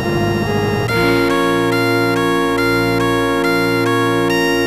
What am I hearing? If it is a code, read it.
There is no code.